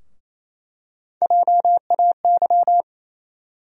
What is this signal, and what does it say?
Morse code, heard as a keyed tone, about 28 wpm, 700 Hz. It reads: JAY
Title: USA Call Sign Postfixes - 3 Letters 28wpm